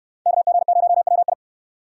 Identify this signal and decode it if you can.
Morse code, heard as a keyed tone, about 68 wpm, 700 Hz. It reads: KG0WI